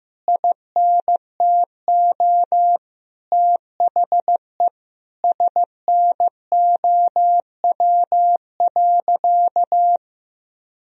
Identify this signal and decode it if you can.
Morse code, heard as a keyed tone, about 15 wpm, 700 Hz. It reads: INTO THE SNOW.
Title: Sentences from Top 1000 Words - Review Part 3 - 15wpm